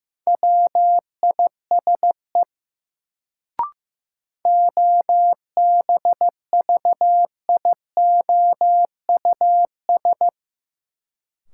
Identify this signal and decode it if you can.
Morse code, heard as a keyed tone, about 15 wpm, 700 Hz. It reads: WISE OBVIOUS
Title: Sentences from Top 1600 Words - New Word Drill - 15wpm